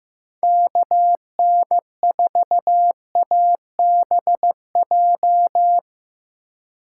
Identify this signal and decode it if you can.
Morse code, heard as a keyed tone, about 15 wpm, 700 Hz. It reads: KN4ABJ
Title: USA Call Signs - 6 Characters - 4x Character Spacing - 15wpm